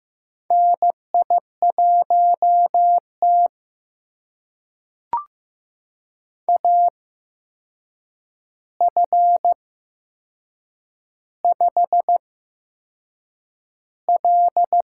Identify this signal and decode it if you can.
Morse code, heard as a keyed tone, about 15 wpm, spaced as for 6 wpm, 700 Hz. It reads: NI1T AF5L
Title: USA Call Signs - 4 Characters - 8x Character Spacing - 15wpm